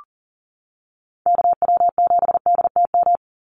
Morse code, heard as a keyed tone, about 40 wpm, 700 Hz. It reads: KW7DTM
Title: USA Call Signs 40wpm